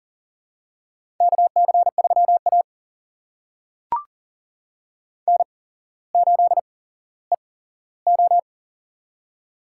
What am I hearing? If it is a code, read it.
Morse code, heard as a keyed tone, about 40 wpm, spaced as for 17 wpm, 700 Hz. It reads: KC3A N8EO
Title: USA Call Signs - 8x Character Spacing - 40wpm